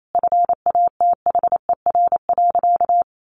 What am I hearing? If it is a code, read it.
Morse code, heard as a keyed tone, about 28 wpm, 700 Hz. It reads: FATHER.
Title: Sentences from Top 700 Words 28wpm